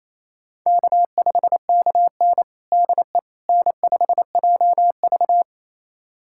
Morse code, heard as a keyed tone, about 28 wpm, 700 Hz. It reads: K5KN DE N5JV
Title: USA Callsign DE Callsign - 4 Characters - 28wpm